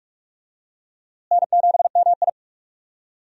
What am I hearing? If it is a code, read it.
Morse code, heard as a keyed tone, about 41 wpm, 700 Hz. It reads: N7MI
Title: USA Call Signs - 4 Characters - Repeated 2x - 45wpm